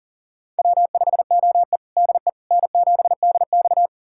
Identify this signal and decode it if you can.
Morse code, heard as a keyed tone, about 40 wpm, 700 Hz. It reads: W5OE DE N7DX
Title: USA Callsign DE Callsign - 4 Characters - 40wpm